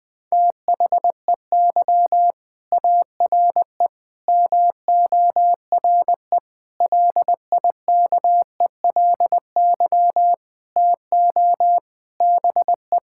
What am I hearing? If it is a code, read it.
Morse code, heard as a keyed tone, about 20 wpm, 700 Hz. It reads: THEY ARE MORE LIKELY TO BE
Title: Sentences with 5-Word Phrases - 20wpm